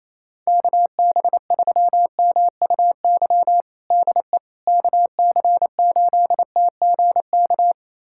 Morse code, heard as a keyed tone, about 28 wpm, 700 Hz. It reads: KB3MUY DE KC8TGK
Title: USA Callsign DE Callsign - 6 Characters - 28wpm